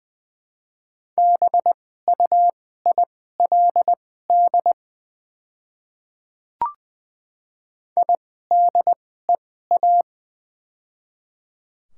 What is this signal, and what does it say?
Morse code, heard as a keyed tone, about 21 wpm, spaced as for 15 wpm, 700 Hz. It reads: BUILD IDEA
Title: 2x Character Spacing - Top 300 Words - 20wpm